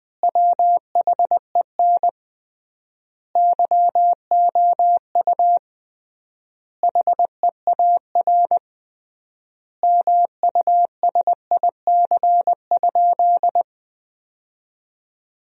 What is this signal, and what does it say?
Morse code, heard as a keyed tone, about 20 wpm, 700 Hz. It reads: WHEN YOU HEAR MUSIC?
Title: 3x Word-Spacing - Sentences from Top 300 Words - 20wpm